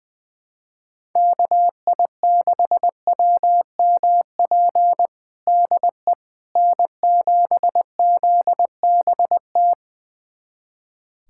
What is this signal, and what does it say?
Morse code, heard as a keyed tone, about 21 wpm, 700 Hz. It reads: KI6WMP DE N7ZBT
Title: USA Callsign DE Callsign 20wpm